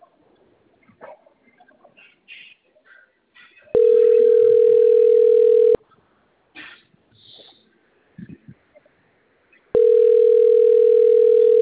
Ringback tone (North America), 440 + 480 Hz.
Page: Campinas